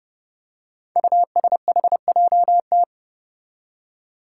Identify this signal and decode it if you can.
Morse code, heard as a keyed tone, about 30 wpm, 700 Hz. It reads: USHJT
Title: Five Challenging Letter-Number Combinations 30wpm